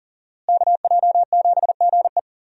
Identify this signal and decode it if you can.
Morse code, heard as a keyed tone, about 40 wpm, 700 Hz. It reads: KJ7GE
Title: USA Call Signs - 5 Characters - Repeated 3x - 40wpm